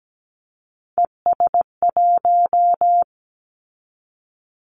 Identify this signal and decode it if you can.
Morse code, heard as a keyed tone, about 17 wpm, 700 Hz. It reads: ES1
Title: Callsign Prefixes for Top 50 DXCC - 17wpm